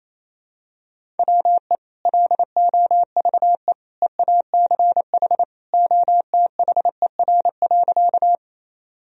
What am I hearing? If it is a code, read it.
Morse code, heard as a keyed tone, about 28 wpm, 700 Hz. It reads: WE LOVE EACH OTHER.
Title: Sentences from Top 1000 Words - Review Part 1 - 28wpm